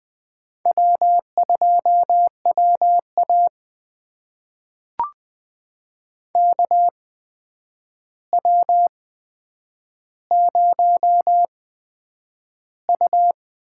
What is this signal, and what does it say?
Morse code, heard as a keyed tone, about 20 wpm, spaced as for 9 wpm, 700 Hz. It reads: W2WA KW0U
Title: USA Call Signs - 4 Characters - 8x Character Spacing - 20wpm